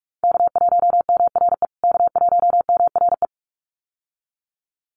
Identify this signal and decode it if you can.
Morse code, heard as a keyed tone, about 44 wpm, 700 Hz. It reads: K1MRE K1MRE
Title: USA Call Signs Repeated 2x 45wpm